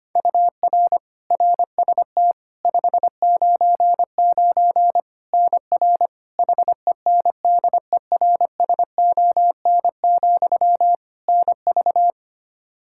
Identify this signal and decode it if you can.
Morse code, heard as a keyed tone, about 25 wpm, 700 Hz. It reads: UR RST 599 NR HENDERSON, NV
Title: Practice QSO - Exchange 1 - Beginning - 25wpm